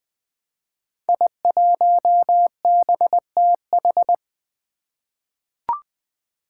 Morse code, heard as a keyed tone, about 21 wpm, 700 Hz. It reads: I1BTH
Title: Five Challenging Letter-Number Combinations 20wpm